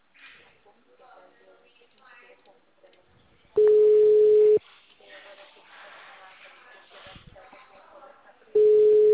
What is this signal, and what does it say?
Ringback tone (Europe, 425 Hz), with speech.